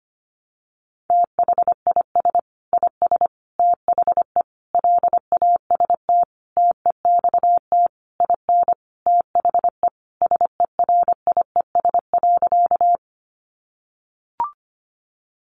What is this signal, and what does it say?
Morse code, heard as a keyed tone, about 25 wpm, 700 Hz. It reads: THIS IS THE LAST TEXT IN THE SERIES.
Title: Sentences from Top 1200 Words 25wpm